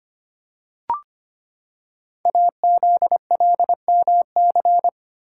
Morse code, heard as a keyed tone, about 25 wpm, 700 Hz. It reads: AZLMC
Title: Code Groups - 5 Letters - 25wpm